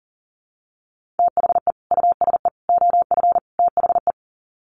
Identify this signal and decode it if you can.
Morse code, heard as a keyed tone, about 40 wpm, 700 Hz. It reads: THE USE OF THE
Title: Sets of 4 Words 40wpm